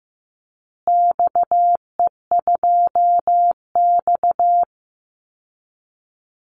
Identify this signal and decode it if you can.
Morse code, heard as a keyed tone, about 15 wpm, 700 Hz. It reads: XE2X
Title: Call Signs Repeated 3x - 15wpm